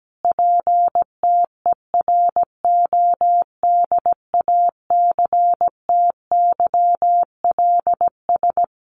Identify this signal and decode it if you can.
Morse code, heard as a keyed tone, about 17 wpm, 700 Hz. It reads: PTERODACTYLS
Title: A Sound Of Thunder 17wpm